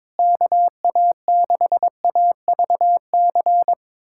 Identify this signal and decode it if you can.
Morse code, heard as a keyed tone, about 22 wpm, 700 Hz. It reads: KA6AVC
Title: USA Callsign DE Callsign - 6 Characters - 22wpm